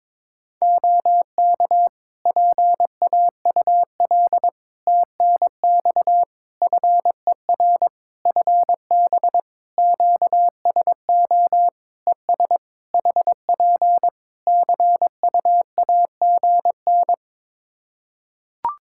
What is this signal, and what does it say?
Morse code, heard as a keyed tone, about 22 wpm, 700 Hz. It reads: OK PAUL TNX FER FB QSO ES HP CUAGN
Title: K1USN SST QSO Practice 22wpm